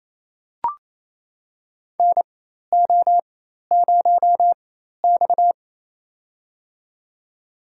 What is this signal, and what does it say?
Morse code, heard as a keyed tone, about 27 wpm, spaced as for 13 wpm, 700 Hz. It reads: NO0X
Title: USA Call Signs - 4 Characters - 4x Character Spacing - 28wpm